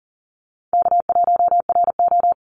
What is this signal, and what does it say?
Morse code, heard as a keyed tone, about 40 wpm, 700 Hz. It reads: K1RO
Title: Easy Call Signs Repeated 3x - 40wpm